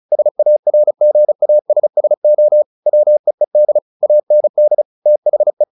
Morse code, heard as a keyed tone, about 34 wpm, 600 Hz. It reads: SARGASSO WEED AND THE
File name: the old man and the sea / the old man and the sea 1 35wpm